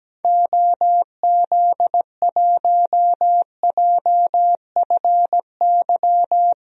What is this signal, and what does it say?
Morse code, heard as a keyed tone, about 17 wpm, 700 Hz. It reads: OZ1JFY
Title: DXCC-100 Call Signs - 6 Characters - 17wpm